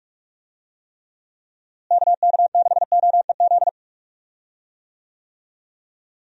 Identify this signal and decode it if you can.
Morse code, heard as a keyed tone, about 41 wpm, 700 Hz. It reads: KK6OEZ
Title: USA Call Signs - 6 Characters - Repeated 3x - 45wpm